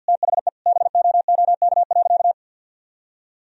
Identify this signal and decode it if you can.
Morse code, heard as a keyed tone, about 51 wpm, 700 Hz. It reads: THE BOOK.
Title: Sentences from Top 1000 Words - Review Part 1 - 50wpm